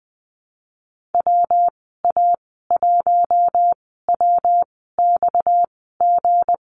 Morse code, heard as a keyed tone, about 20 wpm, spaced as for 14 wpm, 700 Hz. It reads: WA1WXG
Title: 238 Lesson 38g - Letter q - Call Signs 20wpm